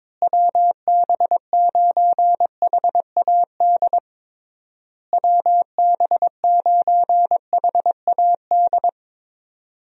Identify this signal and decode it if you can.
Morse code, heard as a keyed tone, about 22 wpm, 700 Hz. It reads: WB9HAD WB9HAD
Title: USA Call Signs - 6 Characters - Repeated 3x - 22wpm